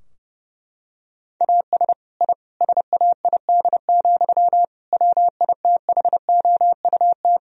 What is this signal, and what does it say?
Morse code, heard as a keyed tone, about 32 wpm, 700 Hz. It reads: AS I SAID, WITHOUT
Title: Sentences from Top 1000 Words - Review Part 3 - 30wpm